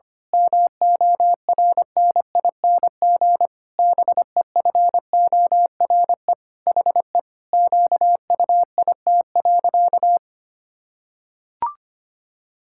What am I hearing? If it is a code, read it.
Morse code, heard as a keyed tone, about 25 wpm, 700 Hz. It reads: MORNING BEFORE HE QUIT.
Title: Pangram Challenge v2 - 25wpm